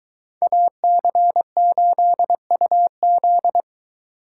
Morse code, heard as a keyed tone, about 20 wpm, 700 Hz. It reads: AC8UZ